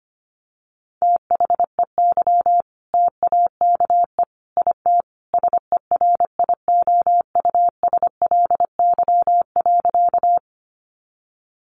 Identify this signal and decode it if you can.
Morse code, heard as a keyed tone, about 25 wpm, 700 Hz. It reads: THEY TAKE IT SERIOUSLY.
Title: Sentences from Top 1200 Words 25wpm